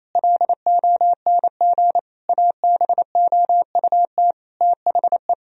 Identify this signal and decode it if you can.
Morse code, heard as a keyed tone, about 28 wpm, 700 Hz. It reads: LONG ABOUT THE
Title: Sentences from Top 1000 Words - Review Part 3 - 28wpm